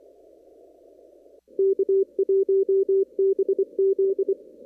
Morse code, heard as a keyed tone, about 24 wpm, 390 Hz. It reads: K1BZ
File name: CW / kt5j wpx cw-op-k5pi